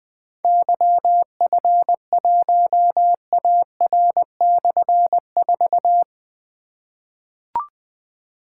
Morse code, heard as a keyed tone, about 20 wpm, 700 Hz. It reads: YF1AR/4
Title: Difficult Call Signs 20wpm